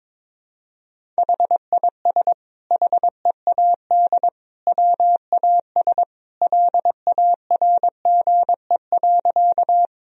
Morse code, heard as a keyed tone, about 22 wpm, 700 Hz. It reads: HIS HEAD WAS LARGE.
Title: Sentences from Top 1000 Words - Review Part 1 - 22wpm